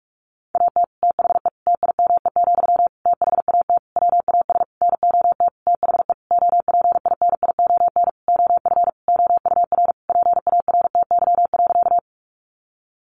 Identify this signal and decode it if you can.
Morse code, heard as a keyed tone, about 44 wpm, 700 Hz. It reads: AT THE TIME, THAT WAS NOT THE OPINION OF OUR PARTY.
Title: Sentences from Top 1400 Words 45wpm